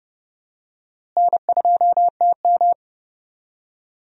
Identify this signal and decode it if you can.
Morse code, heard as a keyed tone, about 30 wpm, 700 Hz. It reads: N2TM
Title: USA Call Signs - 4 Characters - Repeated 3x - 30wpm